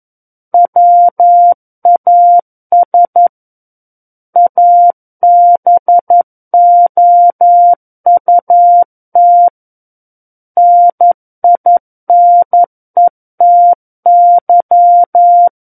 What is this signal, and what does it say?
Morse code, heard as a keyed tone, about 11 wpm, 700 Hz. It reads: WAS ABOUT NINETY
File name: Book One / 02-Chapter 2 - 11 WPM